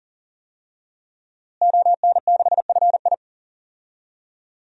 Morse code, heard as a keyed tone, about 37 wpm, 700 Hz. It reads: ON6FI
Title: DXCC-100 Call Signs - 5 Characters - Repeated 3x - 40wpm